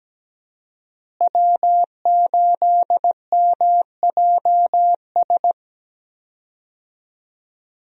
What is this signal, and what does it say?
Morse code, heard as a keyed tone, about 17 wpm, 700 Hz. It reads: W8MJS
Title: USA Call Signs - 5 Characters - Repeated 2x - 17wpm